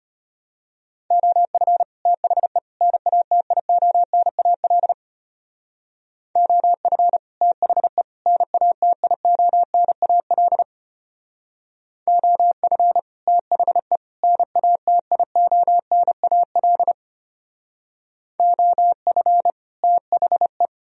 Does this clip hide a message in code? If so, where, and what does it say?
Yes: Morse code from 1.1 to 20.7 s, heard as a keyed tone, about 33 wpm, 700 Hz. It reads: OF THE NATIONAL OF THE NATIONAL OF THE NATIONAL OF THE